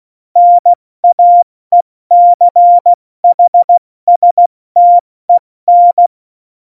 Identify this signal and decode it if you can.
Morse code, heard as a keyed tone, about 16 wpm, 700 Hz. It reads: NAECHSTEN
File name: Morsekurs / Witze / Witz 14 April 2020, 70 F80